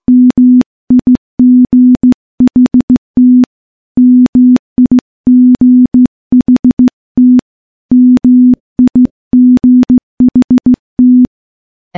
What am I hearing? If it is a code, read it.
Morse code, heard as a keyed tone, about 15 wpm, 250 Hz. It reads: MIGHT MIGHT MIGHT
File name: morse-training-15WPM